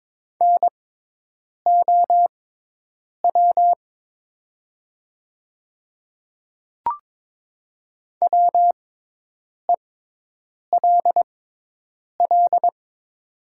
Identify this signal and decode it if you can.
Morse code, heard as a keyed tone, about 22 wpm, spaced as for 8 wpm, 700 Hz. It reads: NOW WELL
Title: 6x Character Spacing - Top 200 Words - 22wpm